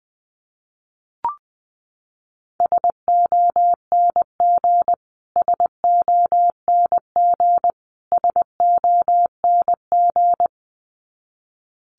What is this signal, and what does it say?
Morse code, heard as a keyed tone, about 20 wpm, 700 Hz. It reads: SONG SONG SONG
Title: Top 500 Words Repeated 20wpm